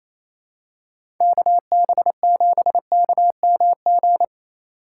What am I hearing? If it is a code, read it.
Morse code, heard as a keyed tone, about 28 wpm, 700 Hz. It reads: KB7KMG
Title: USA Call Signs - 6 Characters - 4x Character Spacing - 28wpm